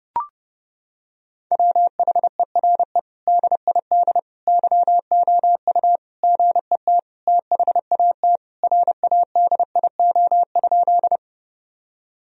Morse code, heard as a keyed tone, about 32 wpm, 700 Hz. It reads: WHERE DID YOU GET THAT RADIO?
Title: Sentences from Top 1000 Words - Review Part 3 - 30wpm